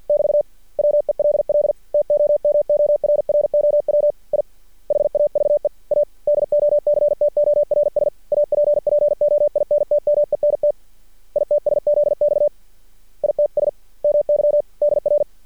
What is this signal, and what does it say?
Morse code, heard as a keyed tone, about 50 wpm, 590 Hz. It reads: <BT> WELL TOMORROW I HAVE A DOCTORS APPOINTMENT ITSZX ITS MY DR
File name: archived / CRSnet